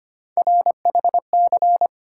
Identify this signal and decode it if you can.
Morse code, heard as a keyed tone, about 25 wpm, 700 Hz. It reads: RHC
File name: Three Letter-Number Combinations - 25wpm